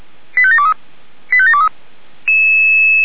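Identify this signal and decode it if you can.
Five-tone selective call (ZVEI), heard as quick runs of five tones, tones of about 70 ms. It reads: 85723 85723
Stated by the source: ZVEI five-tone selective call received by SDR.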